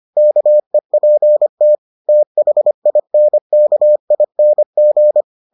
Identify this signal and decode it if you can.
Morse code, heard as a keyed tone, about 25 wpm, 600 Hz. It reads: KEPT THINKING